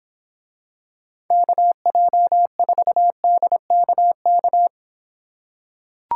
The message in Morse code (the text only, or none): KJ4DKK